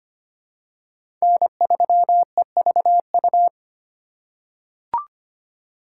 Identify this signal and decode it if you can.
Morse code, heard as a keyed tone, about 25 wpm, 700 Hz. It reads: N3EVU
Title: USA Call Signs - 5 Characters - 4x Character Spacing - 25wpm